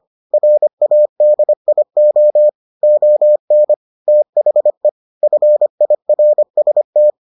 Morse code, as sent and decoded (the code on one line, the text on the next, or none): .-. .- -.. .. --- / --- -. / - .... / ..-. .. .-. ... -
RADIO ON THE FIRST